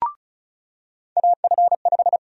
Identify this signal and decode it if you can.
Morse code, heard as a keyed tone, about 34 wpm, 700 Hz. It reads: AF5
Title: USA Call Sign Prefixes - 35wpm